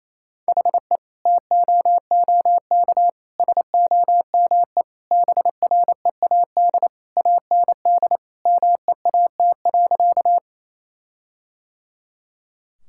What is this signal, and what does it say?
Morse code, heard as a keyed tone, about 28 wpm, 700 Hz. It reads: HE TOOK SOME BREAD AND MEAT.
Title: Sentences from Top 1600 Words 28wpm